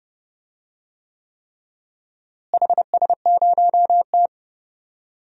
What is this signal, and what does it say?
Morse code, heard as a keyed tone, about 32 wpm, 700 Hz. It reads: HS0T
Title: DXCC-100 Call Signs - 4 Characters - Repeated 3x - 30wpm